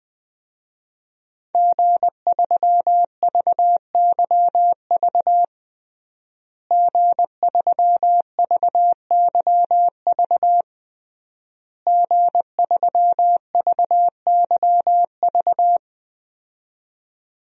Morse code, heard as a keyed tone, about 21 wpm, 700 Hz. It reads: G3VYV G3VYV G3VYV
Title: DXCC-100 Call Signs - 5 Characters - Repeated 3x - 20wpm